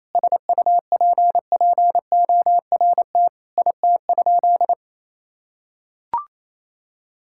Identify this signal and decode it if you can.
Morse code, heard as a keyed tone, about 28 wpm, 700 Hz. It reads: SUPPORT IT?
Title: Sentences from Top 1000 Words - Review Part 5 - 28wpm